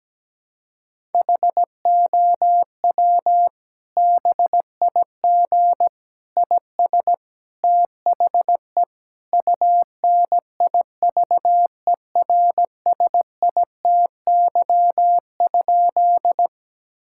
Morse code, heard as a keyed tone, about 17 wpm, 700 Hz. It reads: HOW BIG IS THE UNIVERSITY?
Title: Sentences from Top 1000 Words - Review Part 5 - 17wpm